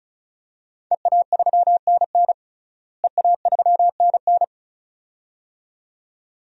Morse code, heard as a keyed tone, about 34 wpm, 700 Hz. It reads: EA3NN EA3NN
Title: DXCC-100 Call Signs - 5 Characters - Repeated 2x - 35wpm